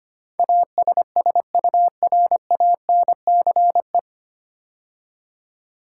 Morse code, heard as a keyed tone, about 25 wpm, 700 Hz. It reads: ASSURANCE
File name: practice / 25wpm / 9 Letter Words - First 3 Letters Given - 25wpm